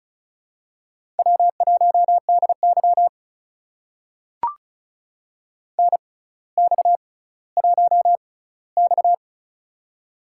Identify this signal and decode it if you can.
Morse code, heard as a keyed tone, about 36 wpm, spaced as for 18 wpm, 700 Hz. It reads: W1DY NX1X
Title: USA Call Signs - 4 Characters - 6x Character Spacing - 35wpm